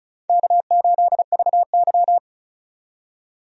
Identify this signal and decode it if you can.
Morse code, heard as a keyed tone, about 34 wpm, 700 Hz. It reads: K8VY
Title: USA Call Signs - 4 Characters - 35wpm